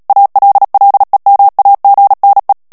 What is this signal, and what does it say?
Morse code, heard as a keyed tone, about 37 wpm, 800 Hz. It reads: ALLEMAGNE
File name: CW / FICHIERS-MP3 / FRANCAIS TEXTE CLAIR 1 VITESSE 35 MOTS MINUTES